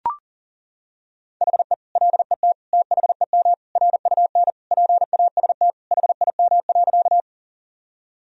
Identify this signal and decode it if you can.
Morse code, heard as a keyed tone, about 37 wpm, 700 Hz. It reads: HE LET THEM RUN PAST HIM.
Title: Sentences from Top 1000 Words - Review Part 3 - 40wpm